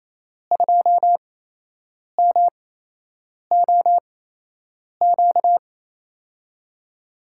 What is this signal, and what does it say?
Morse code, heard as a keyed tone, about 28 wpm, spaced as for 8 wpm, 700 Hz. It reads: 2MOQ
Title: USA Call Signs - 5 Characters - 8x Character Spacing - 28wpm